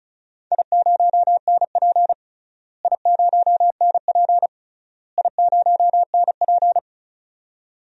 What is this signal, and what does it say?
Morse code, heard as a keyed tone, about 36 wpm, 700 Hz. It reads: I0NP I0NP I0NP
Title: DXCC-100 Call Signs - 4 Characters - Repeated 3x - 35wpm